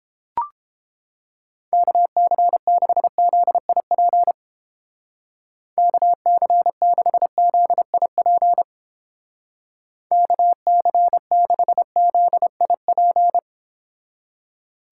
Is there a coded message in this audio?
Yes: Morse code from 1.7 to 13.4 s, heard as a keyed tone, about 31 wpm, 700 Hz. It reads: KC6ZIP KC6ZIP KC6ZIP